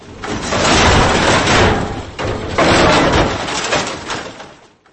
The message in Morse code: none